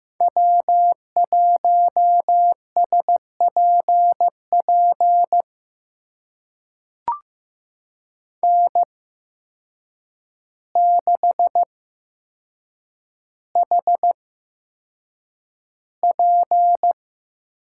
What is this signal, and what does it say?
Morse code, heard as a keyed tone, about 15 wpm, spaced as for 7 wpm, 700 Hz. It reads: W1SPP N6HP
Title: USA Call Signs - 5 Characters - 8x Character Spacing - 15wpm